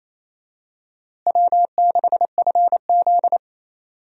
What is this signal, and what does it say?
Morse code, heard as a keyed tone, about 28 wpm, 700 Hz. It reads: W6FZ
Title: USA Call Signs - 4 Characters - Repeated 2x - 28wpm